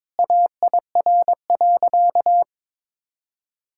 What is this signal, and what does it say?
Morse code, heard as a keyed tone, about 22 wpm, 700 Hz. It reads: AIR.
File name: practice / 22wpm / Sentences with 3-Word Phrases - 22wpm